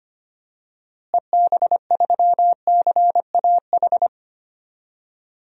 Morse code, heard as a keyed tone, about 25 wpm, 700 Hz. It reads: EB3CAH
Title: DXCC-100 Call Signs - 6 Characters - Repeated 2x - 25wpm